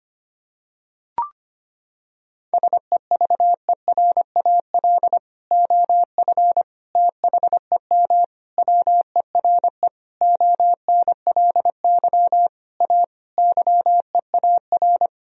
Morse code, heard as a keyed tone, about 25 wpm, 700 Hz. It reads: SEVERAL OF THEM WERE ONLY A YEAR
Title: Sentences from Top 1000 Words - Review Part 2 - 25wpm